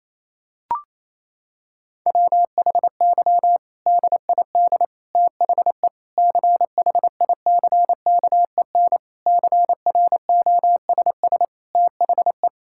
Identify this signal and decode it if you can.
Morse code, heard as a keyed tone, about 28 wpm, 700 Hz. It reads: WHY DID THE CHICKEN CROSS THE
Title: Sentences from Top 1000 Words - Review Part 5 - 28wpm